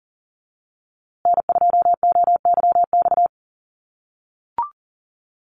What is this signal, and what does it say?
Morse code, heard as a keyed tone, about 40 wpm, 700 Hz. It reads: N2OYX
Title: USA Call Signs Repeated 3x 40wpm